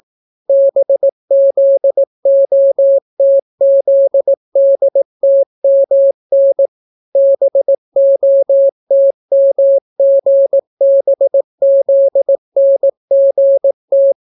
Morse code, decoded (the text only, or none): BZOTZDTMN BOTMGBZNGT